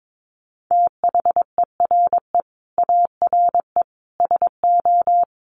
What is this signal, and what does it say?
Morse code, heard as a keyed tone, about 22 wpm, 700 Hz. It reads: THERE ARE SO
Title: Additional Easy Phrases 22wpm